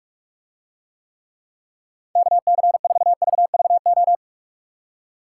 Morse code, heard as a keyed tone, about 44 wpm, 700 Hz. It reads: KC4VVO